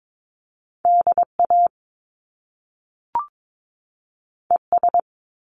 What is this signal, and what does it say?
Morse code, heard as a keyed tone, about 22 wpm, 700 Hz. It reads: DA ES
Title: Top Two Letter Combinations in Words 22wpm